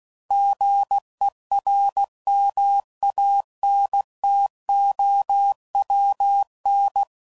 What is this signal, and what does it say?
Morse code, heard as a keyed tone, about 16 wpm, 800 Hz. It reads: GERMANTOWN
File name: CW Code Groups @15WPM